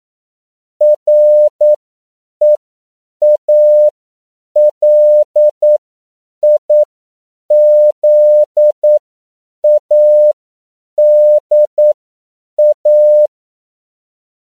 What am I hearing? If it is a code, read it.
Morse code, heard as a keyed tone, about 9 wpm, spaced as for 7 wpm, 600 Hz. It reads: REALIZADA
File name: CW / 10PPM